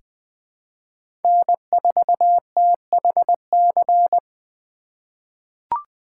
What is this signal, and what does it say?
Morse code, heard as a keyed tone, about 20 wpm, 700 Hz. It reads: N4THC
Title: USA Call Signs - 5 Characters - 4x Character Spacing - 20wpm